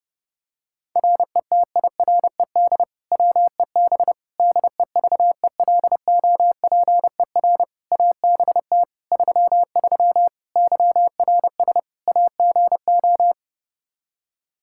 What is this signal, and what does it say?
Morse code, heard as a keyed tone, about 30 wpm, 700 Hz. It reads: RETIRED WEB DEVELOPER ABT 33 YRS AGO
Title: K1USN SST QSO Practice 30wpm